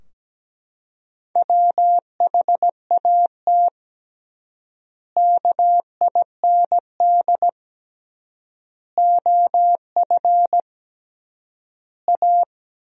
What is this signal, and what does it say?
Morse code, heard as a keyed tone, about 17 wpm, 700 Hz. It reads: WHAT KIND OF A
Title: 3x Word-Spacing - Sentences from Top 400 Words - 17wpm